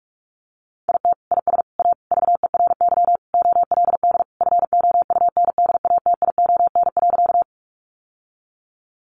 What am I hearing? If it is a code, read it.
Morse code, heard as a keyed tone, about 44 wpm, 700 Hz. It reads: IT IS A VERY OLD FOUNDATION.
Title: Sentences from Top 1200 Words 45wpm